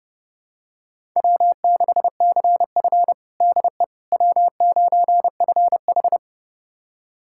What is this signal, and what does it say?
Morse code, heard as a keyed tone, about 32 wpm, 700 Hz. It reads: W6CF DE W9FH
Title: USA Callsign DE Callsign - 4 Characters - 30wpm